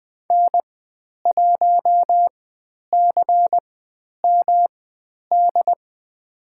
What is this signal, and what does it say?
Morse code, heard as a keyed tone, about 20 wpm, spaced as for 10 wpm, 700 Hz. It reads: N1CMD
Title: 109 Lesson 18g - Letter m - Call Signs 20wpm